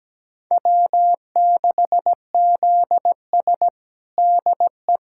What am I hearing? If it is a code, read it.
Morse code, heard as a keyed tone, about 17 wpm, 700 Hz. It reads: W6ZS DE
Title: USA Callsign DE Callsign - 4 Characters - 17wpm